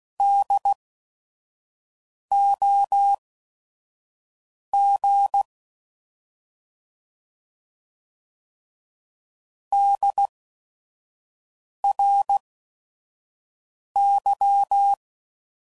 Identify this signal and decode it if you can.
Morse code, heard as a keyed tone, about 16 wpm, spaced as for 5 wpm, 800 Hz. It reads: DOG DRY
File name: Prefixes / By First Letter / Ds